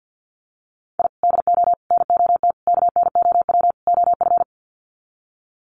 Morse code, heard as a keyed tone, about 50 wpm, 700 Hz. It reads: I DO NOT KNOW OF
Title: Sets of 5 Words Encore2 50wpm